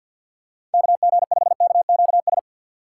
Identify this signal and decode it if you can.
Morse code, heard as a keyed tone, about 53 wpm, 700 Hz. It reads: K G 5 K Q S
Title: USA Call Signs - 6 Characters - 6x Character Spacing - 50wpm